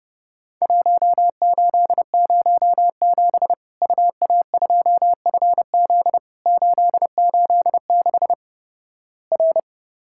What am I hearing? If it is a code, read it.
Morse code, heard as a keyed tone, about 30 wpm, 700 Hz. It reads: 1807 UA2FZ 886 R
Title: Worked All Europe DX Contest - QTC Exchange - 30wpm